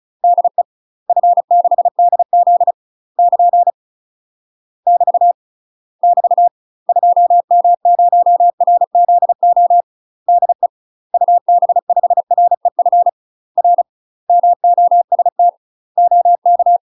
Morse code, heard as a keyed tone, about 34 wpm, 700 Hz. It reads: DE F6DZ <KN> <BT> <BT> 2M0RZO DE UB5REF R MOST OK
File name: UK MASTER QSOs TRAINING / 35WPM / UK MASTER QSOs 0001